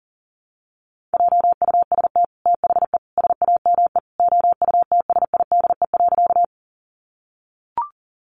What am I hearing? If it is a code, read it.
Morse code, heard as a keyed tone, about 40 wpm, 700 Hz. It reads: JUST THE SAME OUTSIDE.
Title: Sentences from Top 900 Words 40wpm